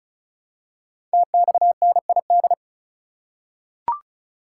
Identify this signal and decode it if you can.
Morse code, heard as a keyed tone, about 36 wpm, 700 Hz. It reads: TXNID
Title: Code Groups - 5 Letter:Numbers - 35wpm